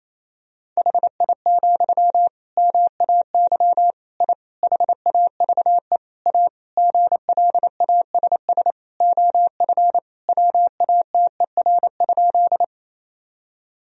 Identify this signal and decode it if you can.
Morse code, heard as a keyed tone, about 27 wpm, 700 Hz. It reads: HI, MAY I HAVE A GLASS OF WATER?
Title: Sentences from Top 1000 Words - Review Part 3 - 28wpm